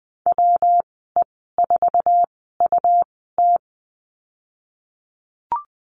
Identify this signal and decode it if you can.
Morse code, heard as a keyed tone, about 20 wpm, spaced as for 15 wpm, 700 Hz. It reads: WE4UT
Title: 128 Lesson 21g - Number 6 - Call Signs 20wpm